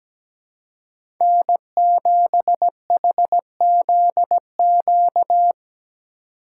Morse code, heard as a keyed tone, about 17 wpm, 700 Hz. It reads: N7HZQ
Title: USA Call Signs - 4x Character Spacing - 17wpm